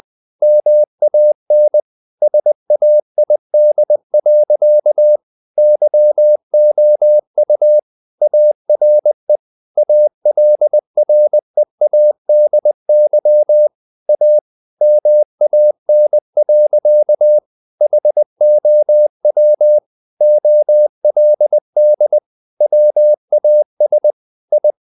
Morse code, heard as a keyed tone, about 20 wpm, 600 Hz. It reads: MAN SAID. YOU ARE ALREADY A MAN. HOW OLD WAS I
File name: the old man and the sea / the old man and the sea part 1 20wpm